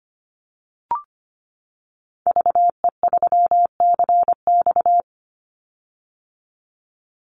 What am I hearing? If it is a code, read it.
Morse code, heard as a keyed tone, about 25 wpm, 700 Hz. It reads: VE3CX
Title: Call Signs - 25wpm